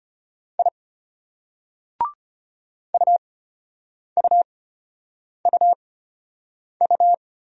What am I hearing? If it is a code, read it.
Morse code, heard as a keyed tone, about 31 wpm, spaced as for 8 wpm, 700 Hz. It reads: I UUUU
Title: Speed-Racing Single Letters 25wpm